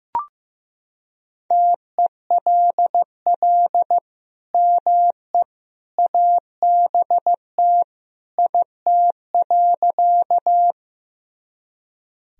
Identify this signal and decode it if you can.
Morse code, heard as a keyed tone, about 15 wpm, 700 Hz. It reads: TELL ME ABT IT.
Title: Sentences from Top 200 Words and Abbreviations 15wpm